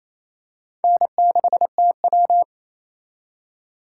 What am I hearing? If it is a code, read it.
Morse code, heard as a keyed tone, about 28 wpm, 700 Hz. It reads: N6TW
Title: K1USN SST QSO Practice - 28wpm